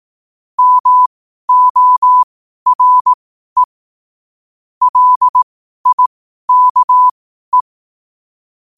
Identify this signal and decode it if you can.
Morse code, heard as a keyed tone, about 18 wpm, spaced as for 12 wpm, 1000 Hz. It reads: MORE LIKE
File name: qotdfiles / 12wpm